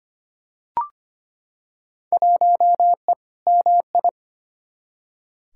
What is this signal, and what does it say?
Morse code, heard as a keyed tone, about 25 wpm, 700 Hz. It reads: 1E MI
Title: Field Day Exchanges - 25wpm